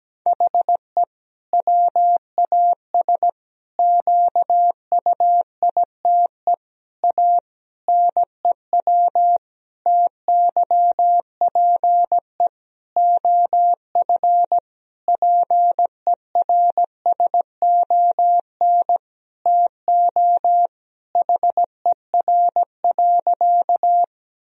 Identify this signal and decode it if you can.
Morse code, heard as a keyed tone, about 17 wpm, 700 Hz. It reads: HE WAS QUITE A NEW TYPE OF PERSON TO HER.
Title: Sentences from Top 1000 Words - Review Part 3 - 17wpm